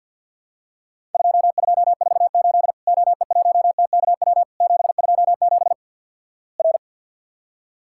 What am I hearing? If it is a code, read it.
Morse code, heard as a keyed tone, about 50 wpm, 700 Hz. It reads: 2248 OE1TKW 727 R
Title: Worked All Europe DX Contest - QTC Exchange - 50wpm